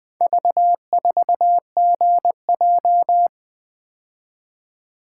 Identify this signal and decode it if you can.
Morse code, heard as a keyed tone, about 20 wpm, 700 Hz. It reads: V4GJ
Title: Four Letter-Number Combinations - 20wpm